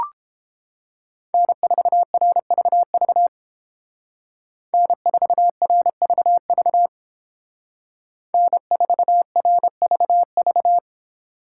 Morse code, heard as a keyed tone, about 28 wpm, 700 Hz. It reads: N4RVV N4RVV N4RVV